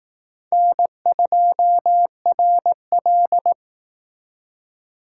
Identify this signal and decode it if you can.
Morse code, heard as a keyed tone, about 18 wpm, 700 Hz. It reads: N2RL
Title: Speed Racing - USA 1x2 Call Signs 15wpm